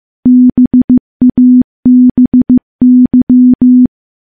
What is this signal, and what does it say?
Morse code, heard as a keyed tone, about 15 wpm, 250 Hz. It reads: BABY
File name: morse-training-15WPM